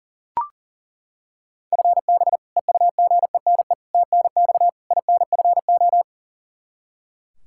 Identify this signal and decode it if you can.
Morse code, heard as a keyed tone, about 40 wpm, 700 Hz. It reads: FB EUGENE TNX INFO
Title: Practice QSO - Exchange 3 - More Info - 40wpm